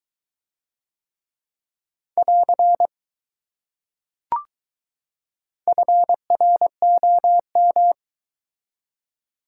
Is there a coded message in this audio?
Yes: Morse code from 2.2 to 7.9 s, heard as a keyed tone, about 20 wpm, 700 Hz. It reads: <AR> FROM